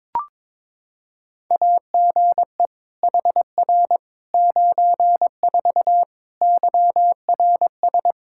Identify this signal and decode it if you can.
Morse code, heard as a keyed tone, about 22 wpm, 700 Hz. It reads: AGE HR 94 YRS
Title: Practice QSO - Exchange 3 - More Info - 22wpm